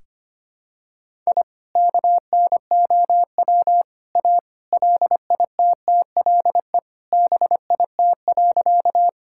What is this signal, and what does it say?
Morse code, heard as a keyed tone, about 25 wpm, 700 Hz. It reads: I KNOW A LITTLE BIT.
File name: practice / 25wpm / Sentences with 3-Word Phrases - 25wpm